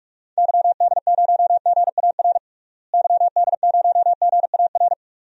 Morse code, heard as a keyed tone, about 46 wpm, 700 Hz. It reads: YD0GAR YD0GAR
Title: DXCC-100 Call Signs - 6 Characters - Repeated 2x - 45wpm